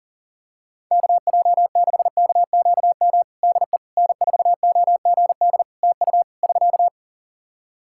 Morse code, heard as a keyed tone, about 40 wpm, 700 Hz. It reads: KJ6KQM DE N4OGD TU <SK>
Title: Practice QSO - Exchange 4 - Ending - 40wpm